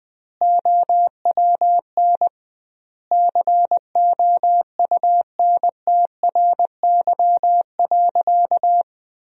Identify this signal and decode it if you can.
Morse code, heard as a keyed tone, about 20 wpm, 700 Hz. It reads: OWN COUNTRY.
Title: 2x Word-Spacing - Sentences from Top 300 Words - 20wpm